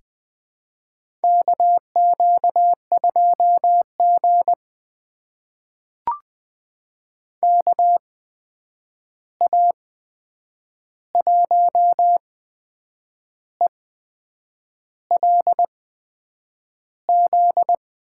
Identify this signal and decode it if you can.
Morse code, heard as a keyed tone, about 20 wpm, spaced as for 7 wpm, 700 Hz. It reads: KQ2G KA1ELZ
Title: USA Call Signs - 8x Character Spacing - 20wpm